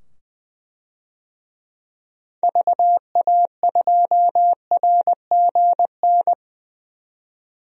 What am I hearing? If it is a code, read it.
Morse code, heard as a keyed tone, about 20 wpm, 700 Hz. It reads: VA2RGN
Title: DXCC-100 Call Signs - 6 Characters - 20wpm